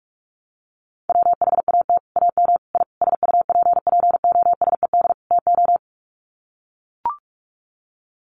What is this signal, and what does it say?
Morse code, heard as a keyed tone, about 44 wpm, 700 Hz. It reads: WHAT AM I SUPPOSED TO
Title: Sets of 5 Words Encore2 45wpm